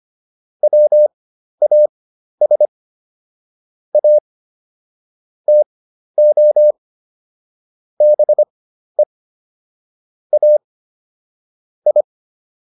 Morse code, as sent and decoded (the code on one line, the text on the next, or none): .-- .- ... / .- / - --- / -... / .- / ..
WAS A TO BE A I